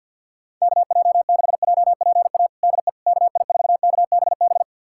Morse code, heard as a keyed tone, about 52 wpm, 700 Hz. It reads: KJ6JPA DE KI4KBB
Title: USA Callsign DE Callsign - 6 Characters - 50wpm